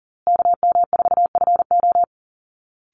Morse code, heard as a keyed tone, about 40 wpm, 700 Hz. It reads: KM4FO
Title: Call Signs Repeated 2x - 40wpm